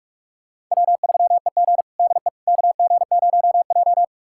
Morse code, heard as a keyed tone, about 46 wpm, 700 Hz. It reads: W3EG DE KG0J
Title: USA Callsign DE Callsign - 4 Characters - 45wpm